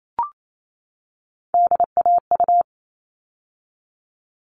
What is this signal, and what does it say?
Morse code, heard as a keyed tone, about 28 wpm, 700 Hz. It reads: DAU
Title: All Cut Numbers - 3-Digit 28wpm